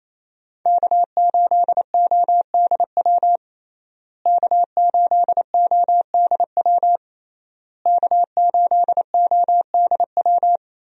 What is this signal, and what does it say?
Morse code, heard as a keyed tone, about 28 wpm, 700 Hz. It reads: K8ODW K8ODW K8ODW
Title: USA Call Signs - 5 Characters - Repeated 3x - 28wpm